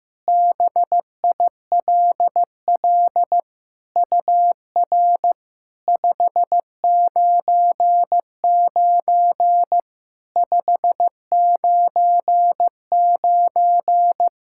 Morse code, heard as a keyed tone, about 16 wpm, 690 Hz. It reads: BILL UR 599 599
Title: SOTA QSO Practice 15wpm